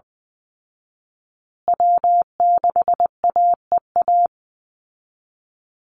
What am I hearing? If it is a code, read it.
Morse code, heard as a keyed tone, about 20 wpm, 700 Hz. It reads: W6AEA
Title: Call Signs Repeated 2x - 20wpm